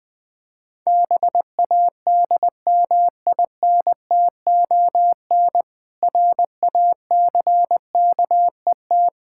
Morse code, heard as a keyed tone, about 20 wpm, 700 Hz. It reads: BADMINTON RACKET
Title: Difficult Words with Hints - 20wpm